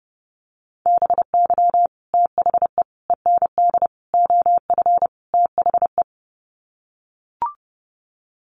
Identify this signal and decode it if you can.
Morse code, heard as a keyed tone, about 28 wpm, 700 Hz. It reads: BY THE END OF THE
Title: Sets of 5 Words 30wpm